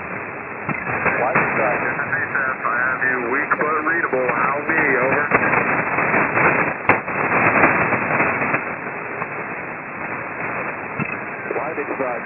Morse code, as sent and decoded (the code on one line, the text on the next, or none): none